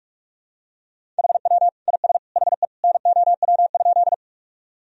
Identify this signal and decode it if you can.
Morse code, heard as a keyed tone, about 46 wpm, 700 Hz. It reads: HW IS HE NOW?
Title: Sentences from Top 100 Words and Abbreviations 45wpm